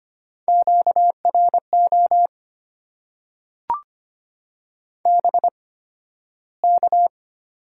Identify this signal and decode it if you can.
Morse code, heard as a keyed tone, about 25 wpm, spaced as for 13 wpm, 700 Hz. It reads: QRO BK
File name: Top 90 QSO Elements - 8x Character Spacing - 25wpm